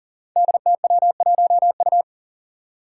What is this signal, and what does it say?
Morse code, heard as a keyed tone, about 40 wpm, 700 Hz. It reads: DTW1U
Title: Five Challenging Letter-Number Combinations 40wpm